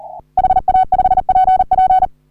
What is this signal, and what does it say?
Morse code, heard as a keyed tone, about 39 wpm, 750 Hz. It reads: HA5PP
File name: CW / HA3OU RHOBCW REC